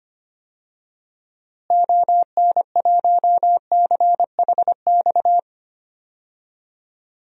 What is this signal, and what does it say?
Morse code, heard as a keyed tone, about 25 wpm, 700 Hz. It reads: ON1CHX
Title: DXCC-100 Call Signs - 6 Characters - Repeated 2x - 25wpm